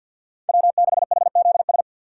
Morse code, heard as a keyed tone, about 49 wpm, 700 Hz. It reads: W6HZS
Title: USA Call Signs - 5 Characters - 6x Character Spacing - 50wpm